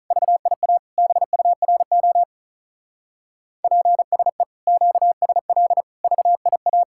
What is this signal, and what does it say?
Morse code, heard as a keyed tone, about 33 wpm, 700 Hz. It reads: VIA BURO PSE QSL VIA
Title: Speed-Racing Common QSO Phrases - 35wpm